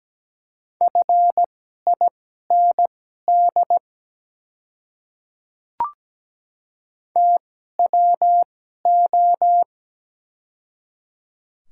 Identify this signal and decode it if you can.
Morse code, heard as a keyed tone, about 17 wpm, spaced as for 12 wpm, 700 Hz. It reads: FIND TWO
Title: 2x Character Spacing - Top 300 Words - 17wpm